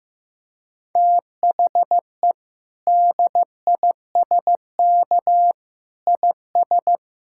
Morse code, heard as a keyed tone, about 16 wpm, 700 Hz. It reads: THE DISK IS
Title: Sentences from Top 1000 Words - Review Part 6 - 15wpm